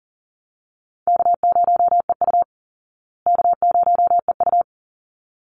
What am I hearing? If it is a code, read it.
Morse code, heard as a keyed tone, about 40 wpm, 700 Hz. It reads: K0EU K0EU
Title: Call Signs Repeated 3x - 40wpm